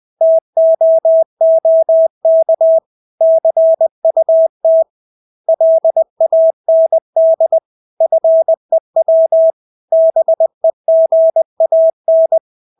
Morse code, heard as a keyed tone, about 20 wpm, 650 Hz. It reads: TOOK CUT LAND FEW BEGAN
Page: Engelske ord 20 wpm